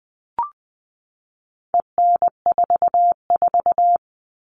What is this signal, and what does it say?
Morse code, heard as a keyed tone, about 20 wpm, 700 Hz. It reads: EN44
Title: All Cut Numbers - 4-Digit 20wpm